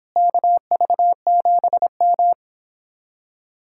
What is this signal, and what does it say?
Morse code, heard as a keyed tone, about 26 wpm, 700 Hz. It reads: KV7M